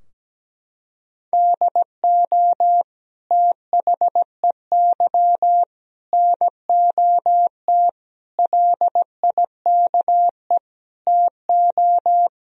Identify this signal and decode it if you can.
Morse code, heard as a keyed tone, about 17 wpm, 700 Hz. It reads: DO THEY NOT LIKE TO
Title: Sentences from Top 1000 Words - Review Part 4 - 17wpm